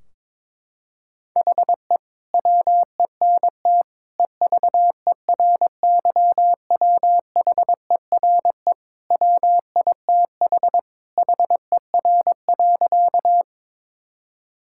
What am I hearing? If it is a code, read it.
Morse code, heard as a keyed tone, about 22 wpm, 700 Hz. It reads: HE WENT EVERYWHERE WITH HER.
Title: Sentences from Top 1600 Words 22wpm